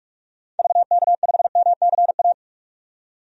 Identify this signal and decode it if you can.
Morse code, heard as a keyed tone, about 42 wpm, 700 Hz. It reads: VK5MCA
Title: DXCC-100 Call Signs - 6 Characters - 45wpm